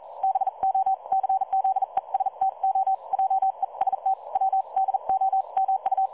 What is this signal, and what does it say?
Morse code, heard as a keyed tone, about 40 wpm, 770 Hz. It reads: BO CZESTO JEST W DOMU
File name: dlazaawansowanych / nagraniazpasma / websdr recording 3565.0kHz